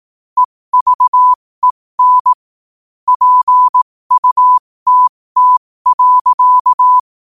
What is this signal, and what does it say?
Morse code, heard as a keyed tone, about 18 wpm, spaced as for 15 wpm, 1000 Hz. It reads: EVEN PUTT.
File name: 2024-11-18-15wpm